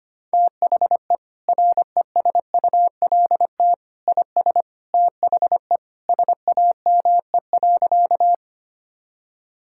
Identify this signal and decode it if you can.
Morse code, heard as a keyed tone, about 25 wpm, 700 Hz. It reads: THE RESULT IS THE SAME.
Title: Sentences from Top 1000 Words - Review Part 5 - 25wpm